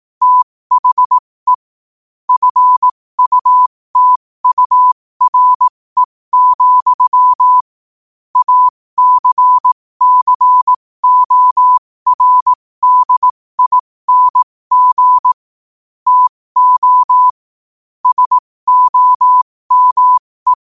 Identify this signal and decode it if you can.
Morse code, heard as a keyed tone, about 18 wpm, spaced as for 15 wpm, 1000 Hz. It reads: THE FUTURE, ACCORDING TO SOME
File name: qotdfiles / 15wpm